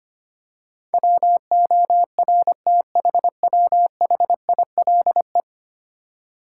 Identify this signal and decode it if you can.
Morse code, heard as a keyed tone, about 25 wpm, 700 Hz. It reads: WORTHWHILE
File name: Ten Letter Words - 25wpm